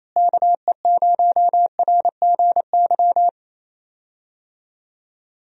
Morse code, heard as a keyed tone, about 28 wpm, 700 Hz. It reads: KE0RGY
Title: USA Call Signs - 6 Characters - Repeated 2x - 28wpm